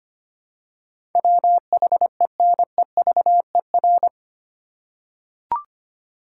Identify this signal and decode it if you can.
Morse code, heard as a keyed tone, about 25 wpm, 700 Hz. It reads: WHENEVER
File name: Eight Letter Words - 25wpm